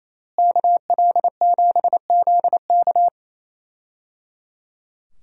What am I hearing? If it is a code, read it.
Morse code, heard as a keyed tone, about 28 wpm, 700 Hz. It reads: KL7ZK
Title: USA Call Signs - 5 Characters - Repeated 3x - 28wpm